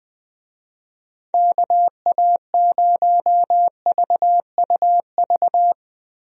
Morse code, heard as a keyed tone, about 21 wpm, 700 Hz. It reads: KA0VUV
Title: USA Call Signs - 6 Characters - Repeated 2x - 20wpm